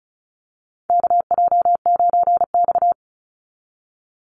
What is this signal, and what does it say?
Morse code, heard as a keyed tone, about 34 wpm, 700 Hz. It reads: KJ9X
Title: NA SOTA Chaser Callsigns 35wpm